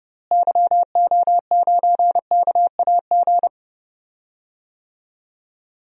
Morse code, heard as a keyed tone, about 30 wpm, 700 Hz. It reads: YO9KAG
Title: DXCC-100 Call Signs - 6 Characters - Repeated 3x - 30wpm